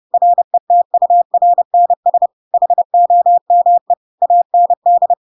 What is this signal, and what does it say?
Morse code, heard as a keyed tone, about 30 wpm, 700 Hz. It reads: RETURNS HOME AND
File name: Pinocchio / Pinocchio ch7 30 30 ik1hge